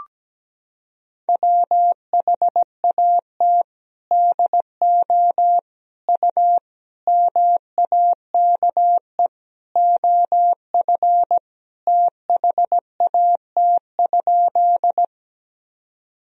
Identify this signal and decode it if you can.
Morse code, heard as a keyed tone, about 17 wpm, 700 Hz. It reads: WHAT DO U MAKE OF THAT?
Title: Sentences from Top 100 Words and Abbreviations 17wpm